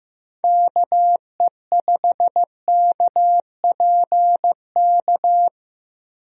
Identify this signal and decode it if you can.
Morse code, heard as a keyed tone, about 15 wpm, 700 Hz. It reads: KE5KPK
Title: USA Call Signs - 6 Characters - Repeated 2x - 15wpm